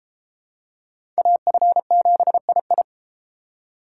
Morse code, heard as a keyed tone, about 29 wpm, 700 Hz. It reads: AF7II